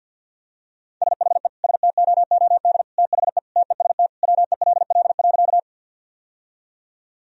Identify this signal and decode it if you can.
Morse code, heard as a keyed tone, about 50 wpm, 700 Hz. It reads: SHE STOOD THE TEST WELL.
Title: Sentences from Top 1000 Words - Review Part 3 - 50wpm